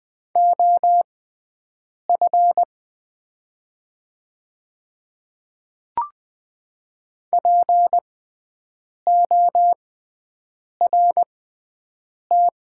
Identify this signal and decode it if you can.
Morse code, heard as a keyed tone, about 20 wpm, spaced as for 7 wpm, 690 Hz. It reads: OF PORT
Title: 6x Character Spacing - Top 200 Words - 20wpm